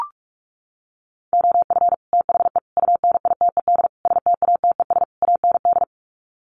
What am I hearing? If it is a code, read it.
Morse code, heard as a keyed tone, about 44 wpm, 700 Hz. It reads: OF THE UNITED STATES AND
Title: Sets of 5 Words 45wpm